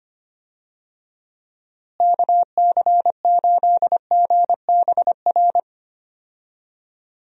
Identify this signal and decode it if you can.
Morse code, heard as a keyed tone, about 25 wpm, 700 Hz. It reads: KC8GBR
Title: USA Call Signs - 6 Characters - Repeated 3x - 25wpm